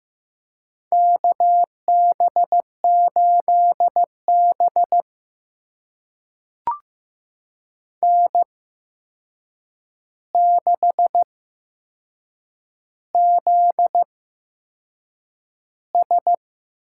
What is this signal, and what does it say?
Morse code, heard as a keyed tone, about 15 wpm, spaced as for 6 wpm, 700 Hz. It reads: KB8B N6ZS
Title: USA Call Signs - 4 Characters - 8x Character Spacing - 15wpm